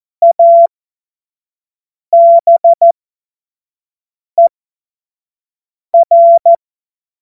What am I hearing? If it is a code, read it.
Morse code, heard as a keyed tone, about 14 wpm, spaced as for 5 wpm, 680 Hz. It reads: ABER
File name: Witz-05-August-2025-25-F70